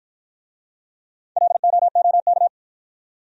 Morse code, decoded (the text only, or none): LOOK